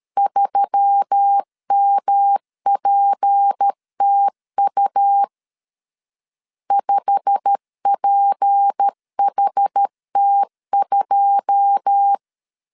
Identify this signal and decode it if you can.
Morse code, heard as a keyed tone, about 13 wpm, 800 Hz. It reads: 3MPTU 5PHT2
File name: LEZIONI / 9 2 12wpm 2301